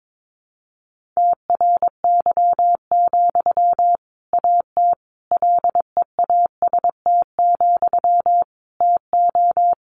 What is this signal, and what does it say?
Morse code, heard as a keyed tone, about 22 wpm, 700 Hz. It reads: TRY, AT LEAST, TO
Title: Sentences from Top 700 Words 22wpm